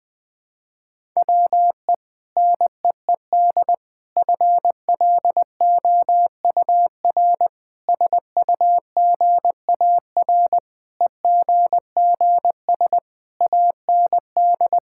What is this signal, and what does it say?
Morse code, heard as a keyed tone, about 20 wpm, 700 Hz. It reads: WE NEED FLOUR SUGAR EGGS AND
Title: Sentences with 6-Word Phrases - Spoken Beginning - 20wpm